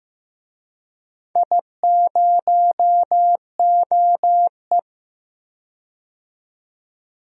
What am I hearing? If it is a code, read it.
Morse code, heard as a keyed tone, about 15 wpm, 700 Hz. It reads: I0OE
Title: DXCC-100 Call Signs - 4 Characters - 15wpm